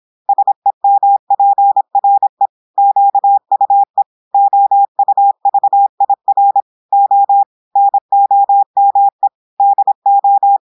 Morse code, heard as a keyed tone, about 26 wpm, 800 Hz. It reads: SEMPRE QUE OUVIR O NOME DO
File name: portugais texte en clair 2 vitesse 26 mots minutes